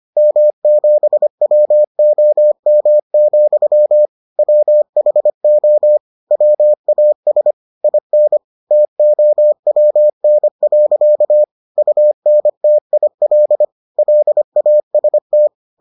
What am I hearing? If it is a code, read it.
Morse code, heard as a keyed tone, about 25 wpm, 600 Hz. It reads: M7WOM, WHO WAS IN TOWN. UNTIL LAST